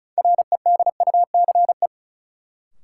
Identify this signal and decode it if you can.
Morse code, heard as a keyed tone, about 36 wpm, 700 Hz. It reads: REDUCE
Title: Sentences from Top 1100 Words - New Word Drill Alt - 35wpm